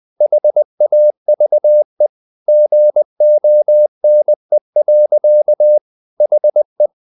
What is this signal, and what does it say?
Morse code, heard as a keyed tone, about 20 wpm, 600 Hz. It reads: HAVE GONE. HE
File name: the old man and the sea / the old man and the sea part 1 20wpm